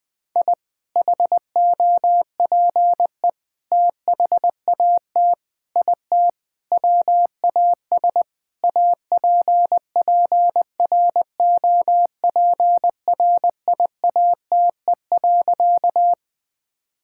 Morse code, heard as a keyed tone, about 21 wpm, 700 Hz. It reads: I HOPE THAT IT WAS APPROPRIATE.
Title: Sentences from Top 1000 Words - Review Part 5 - 20wpm